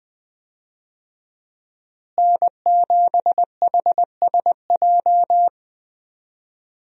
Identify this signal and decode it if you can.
Morse code, heard as a keyed tone, about 21 wpm, 700 Hz. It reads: N7HSJ
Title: USA Call Signs - 5 Characters - Repeated 3x - 20wpm